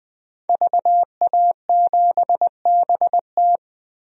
Morse code, heard as a keyed tone, about 21 wpm, 700 Hz. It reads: VA7BT
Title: DXCC-100 Call Signs - 5 Characters - Repeated 2x - 20wpm